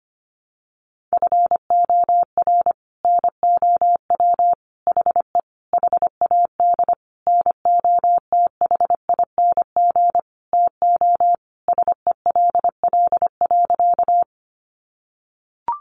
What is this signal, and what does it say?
Morse code, heard as a keyed tone, about 25 wpm, 700 Hz. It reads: FOR NOW HE HAD NOTHING TO SELL.
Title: Sentences from Top 1100 Words 25wpm